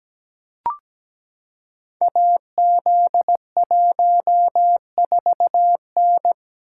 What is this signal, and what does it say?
Morse code, heard as a keyed tone, about 17 wpm, 700 Hz. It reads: AZ14N
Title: Code Groups - 5 Letter:Numbers - 17wpm